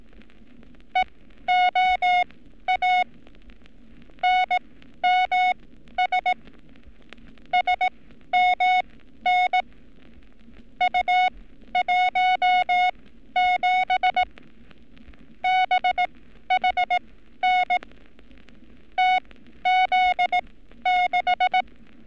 Morse code, heard as a keyed tone, about 18 wpm, spaced as for 12 wpm, 730 Hz. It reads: EOA NMS SMN U17 BHN TZ6
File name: training / NAVPERS 11224-11235 / NAVPERS 11227 RA RB